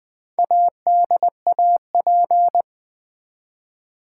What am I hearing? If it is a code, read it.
Morse code, heard as a keyed tone, about 20 wpm, 700 Hz. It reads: ADAP
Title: Build A Word - 1 Letter At A Time - 3 to 5-Letter Words - 20wpm